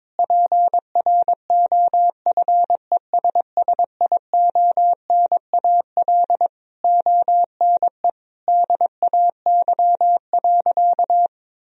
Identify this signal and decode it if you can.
Morse code, heard as a keyed tone, about 22 wpm, 700 Hz. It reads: PROFESSIONAL ONE DAY.
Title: Sentences from Top 1000 Words - Review Part 4 - 22wpm